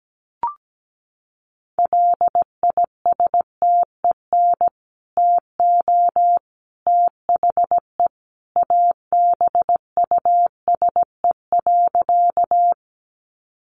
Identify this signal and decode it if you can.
Morse code, heard as a keyed tone, about 17 wpm, 700 Hz. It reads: LISTEN TO THE ABUSE.
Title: Sentences from Top 1300 Words 17wpm